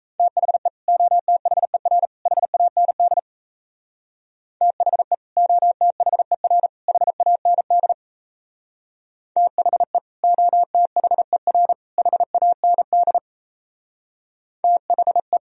Morse code, heard as a keyed tone, about 34 wpm, 700 Hz. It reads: THE OTHER HAND THE OTHER HAND THE OTHER HAND THE
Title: Speed-Racing - Sets of 3 Words 28wpm